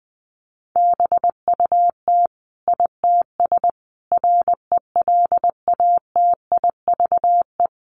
Morse code, heard as a keyed tone, about 20 wpm, 700 Hz. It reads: BUT ITS RELATIVE
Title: Sentences from Top 1400 Words 20wpm